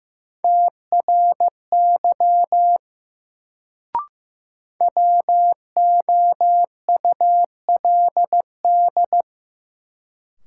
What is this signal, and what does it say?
Morse code, heard as a keyed tone, about 15 wpm, 700 Hz. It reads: TRY WOULD
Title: Sets of 2 Words - New Word Drill - 15wpm